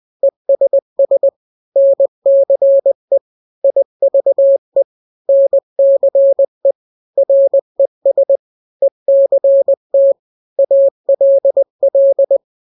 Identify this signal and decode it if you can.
Morse code, heard as a keyed tone, about 20 wpm, 550 Hz. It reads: ESS NCE IVE NCE RES ECT ALL
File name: n-grams / 3-grams 20 18 WPM